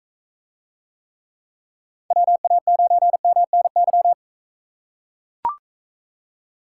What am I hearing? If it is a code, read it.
Morse code, heard as a keyed tone, about 43 wpm, 700 Hz. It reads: WA9MNY